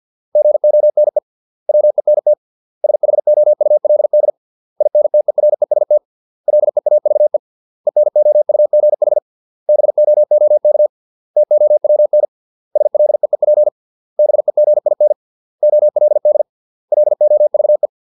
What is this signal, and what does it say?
Morse code, heard as a keyed tone, about 50 wpm, 600 Hz. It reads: GONE WENT SHOULD INTEREST LEAVE ENOUGH BOOK TOWN SLEEP BEGIN OLD LOVE